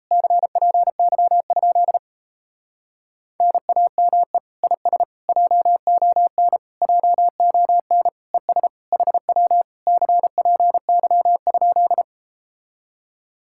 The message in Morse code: CPY? NAME IS JON JON ES HW CPY?

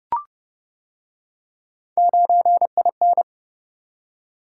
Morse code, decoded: 9IN

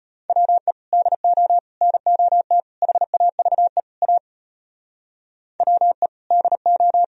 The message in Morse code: WE DO NOT HAVE A WE DO

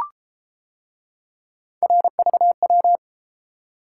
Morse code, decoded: RVW

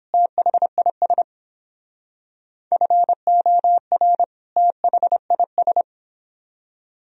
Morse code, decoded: THIS FOR THIS

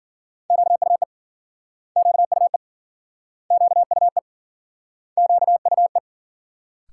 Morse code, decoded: QUE QUE QUE QUE

